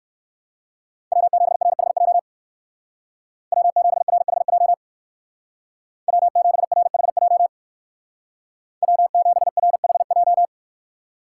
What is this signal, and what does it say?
Morse code, heard as a keyed tone, about 56 wpm, 700 Hz. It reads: W7RHJ W7RHJ W7RHJ W7RHJ
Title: Speed Racing - USA 1x3 Call Signs 45wpm